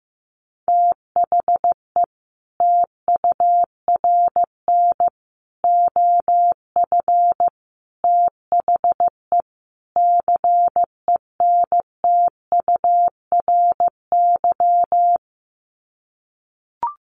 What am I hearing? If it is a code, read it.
Morse code, heard as a keyed tone, about 14 wpm, 700 Hz. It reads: THE TURN OF THE CENTURY
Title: Sets of 5 Words 15wpm